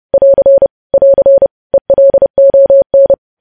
Morse code, heard as a keyed tone, about 30 wpm, 550 Hz. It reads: <AR> <AR> ELON
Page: news headlines in morse code at 30 wpm